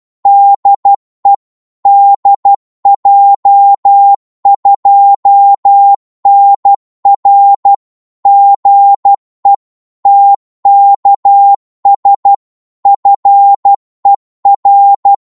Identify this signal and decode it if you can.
Morse code, heard as a keyed tone, about 12 wpm, 800 Hz. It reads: DE DJ2NR GE TKS FER
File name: Morse / qso-12wpm